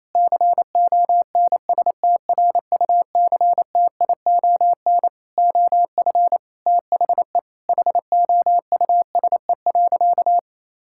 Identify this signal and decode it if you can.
Morse code, heard as a keyed tone, about 28 wpm, 700 Hz. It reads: CONSTRUCTION OF THE HOUSE.
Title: Sentences from Top 1600 Words 28wpm